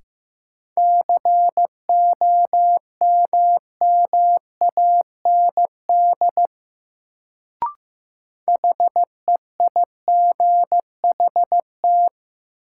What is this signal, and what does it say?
Morse code, heard as a keyed tone, about 15 wpm, 700 Hz. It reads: COMMAND HEIGHT
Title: Sentences from Top 1600 Words - New Word Drill - 15wpm